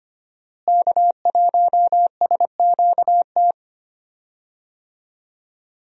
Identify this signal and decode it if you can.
Morse code, heard as a keyed tone, about 25 wpm, 700 Hz. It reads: K1SQT
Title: USA Call Signs - 5 Characters - Repeated 3x - 25wpm